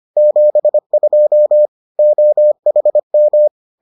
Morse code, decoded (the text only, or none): 72 OHM